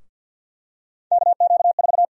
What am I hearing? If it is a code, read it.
Morse code, heard as a keyed tone, about 49 wpm, 700 Hz. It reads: KQ4
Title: USA Call Sign Prefixes - 50wpm